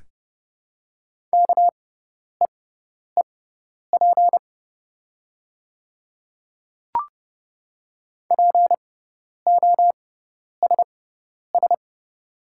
Morse code, heard as a keyed tone, about 30 wpm, spaced as for 10 wpm, 700 Hz. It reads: KEEP POSS
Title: 6x Character Spacing - Top 500 Words - 30wpm